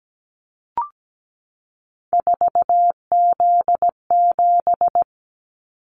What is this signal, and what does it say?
Morse code, heard as a keyed tone, about 17 wpm, 700 Hz. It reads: 4Z7
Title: Callsign Prefixes for Top 50 DXCC - 17wpm